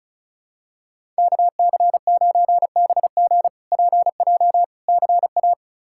Morse code, heard as a keyed tone, about 36 wpm, 700 Hz. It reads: KC9BG PJ CA
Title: K1USN SST QSO Practice - 35wpm